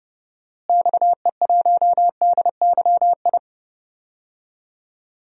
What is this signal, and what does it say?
Morse code, heard as a keyed tone, about 31 wpm, 700 Hz. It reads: XE1DYI
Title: DXCC-100 Call Signs - 6 Characters - Repeated 3x - 30wpm